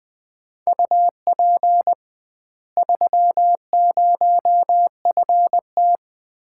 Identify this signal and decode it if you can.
Morse code, heard as a keyed tone, about 20 wpm, 700 Hz. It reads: UP 30FT
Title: Common QSO Phrases - 2x Word-Spacing - 20wpm